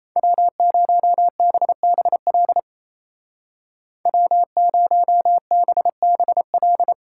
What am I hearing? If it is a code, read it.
Morse code, heard as a keyed tone, about 34 wpm, 700 Hz. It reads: W0BBL W0BBL
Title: Speed Racing - USA 1x3 Call Signs 28wpm